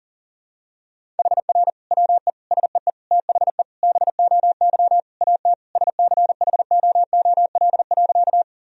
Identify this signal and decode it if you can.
Morse code, heard as a keyed tone, about 40 wpm, 700 Hz. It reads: HR WE SEE THE BOY AT SCHOOL.
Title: Sentences from Top 300 Words and Abbreviations 40wpm